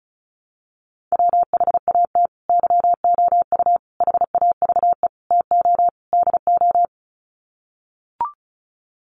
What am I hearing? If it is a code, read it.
Morse code, heard as a keyed tone, about 34 wpm, 700 Hz. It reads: WHAT YOU HAVE TO DO
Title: Sets of 5 Words Encore 35wpm